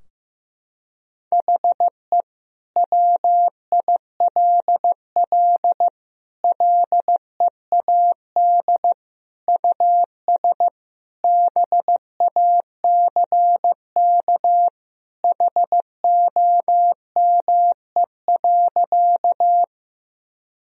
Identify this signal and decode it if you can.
Morse code, heard as a keyed tone, about 15 wpm, 700 Hz. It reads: HE WILL LEAD US BACK HOME.
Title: Sentences from Top 1000 Words - Review Part 2 - 15wpm